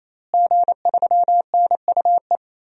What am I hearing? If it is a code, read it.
Morse code, heard as a keyed tone, about 28 wpm, 700 Hz. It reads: G3NUE
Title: DXCC-100 Call Signs - 5 Characters - Repeated 3x - 28wpm